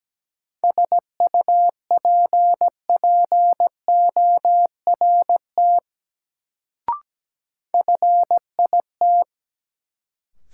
Morse code, heard as a keyed tone, about 17 wpm, 700 Hz. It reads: SUPPORT FIT
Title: Sentences from Top 800 Words - New Word Drill - 17wpm